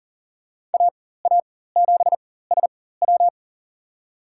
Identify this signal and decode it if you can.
Morse code, heard as a keyed tone, about 40 wpm, spaced as for 19 wpm, 690 Hz. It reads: AA7SW
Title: USA Call Signs - 4x Character Spacing - 40wpm